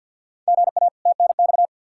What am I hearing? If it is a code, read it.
Morse code, heard as a keyed tone, about 53 wpm, 700 Hz. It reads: GA TNX
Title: Practice QSO - Exchange 1 - Beginning - 50wpm